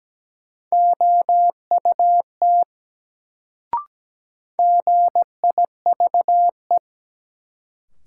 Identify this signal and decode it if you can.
Morse code, heard as a keyed tone, about 17 wpm, 700 Hz. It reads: OUT GIVE
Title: Sets of 2 Words - New Word Drill - 17wpm